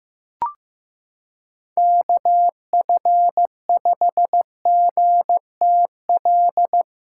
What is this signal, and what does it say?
Morse code, heard as a keyed tone, about 15 wpm, 700 Hz. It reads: KF5GTL
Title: USA Callsign DE Callsign 15wpm